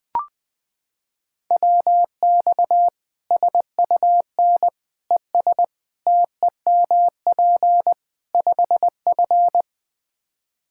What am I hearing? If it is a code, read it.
Morse code, heard as a keyed tone, about 20 wpm, 700 Hz. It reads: WX SUN ES TEMP 5F
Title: Practice QSO - Exchange 2 - Info - 20wpm